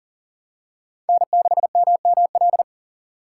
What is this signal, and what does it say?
Morse code, heard as a keyed tone, about 39 wpm, 690 Hz. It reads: N6MML
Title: USA Call Signs - 5 Characters - 3x Character Spacing - 40wpm